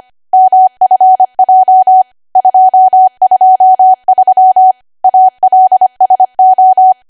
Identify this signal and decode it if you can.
Morse code, heard as a keyed tone, about 25 wpm, 750 Hz. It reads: MFJ 223 ALSO
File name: Archive / 25 WPM / 25WPM